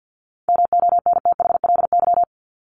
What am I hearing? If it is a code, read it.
Morse code, heard as a keyed tone, about 48 wpm, 700 Hz. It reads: MONTHLY
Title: LY Suffix - Top 100 Words with Base Words 50wpm